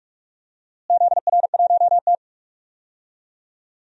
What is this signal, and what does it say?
Morse code, heard as a keyed tone, about 46 wpm, 700 Hz. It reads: ZR1T
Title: DXCC-100 Call Signs - 4 Characters - Repeated 3x - 45wpm